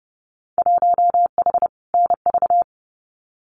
Morse code, heard as a keyed tone, about 30 wpm, 700 Hz. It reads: 1H NV
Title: Winter Field Day Exchanges 30wpm